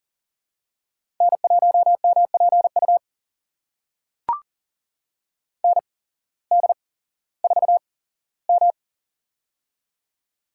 Morse code, heard as a keyed tone, about 39 wpm, spaced as for 19 wpm, 700 Hz. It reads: N1MPU ND4M
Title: USA Call Signs - 8x Character Spacing - 40wpm